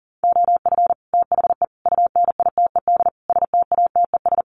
Morse code, heard as a keyed tone, about 40 wpm, 700 Hz. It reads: OF THE UNITED STATES
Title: Sets of 4 Words 40wpm